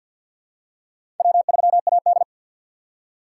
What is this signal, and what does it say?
Morse code, heard as a keyed tone, about 52 wpm, 700 Hz. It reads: W3AD
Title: USA Call Signs - 4 Characters - 6x Character Spacing - 50wpm